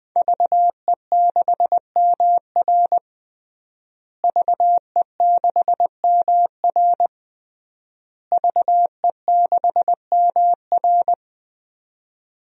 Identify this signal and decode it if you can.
Morse code, heard as a keyed tone, about 20 wpm, 700 Hz. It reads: VE6MR VE6MR VE6MR
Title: DXCC-100 Call Signs - 5 Characters - Repeated 3x - 20wpm